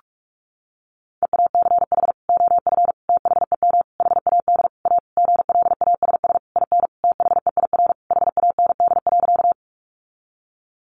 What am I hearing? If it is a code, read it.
Morse code, heard as a keyed tone, about 44 wpm, 700 Hz. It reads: EACH OF THEM HAD A GLASS IN THEIR HAND.
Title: Sentences from Top 1300 Words 45wpm